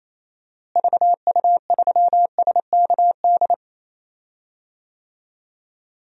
Morse code, heard as a keyed tone, about 28 wpm, 700 Hz. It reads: VU3SKD
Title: DXCC-100 Call Signs - 6 Characters - Repeated 2x - 28wpm